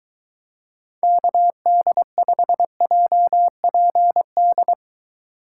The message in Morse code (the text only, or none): KD5JPD